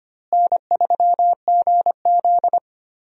Morse code, heard as a keyed tone, about 25 wpm, 700 Hz. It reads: N3GZ
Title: USA Call Signs - 4 Characters - Repeated 2x - 25wpm